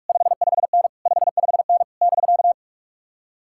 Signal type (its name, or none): Morse code